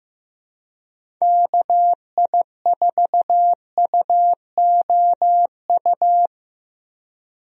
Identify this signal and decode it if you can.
Morse code, heard as a keyed tone, about 15 wpm, 700 Hz. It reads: KI4UOU
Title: USA Call Signs - 6 Characters - Repeated 3x - 15wpm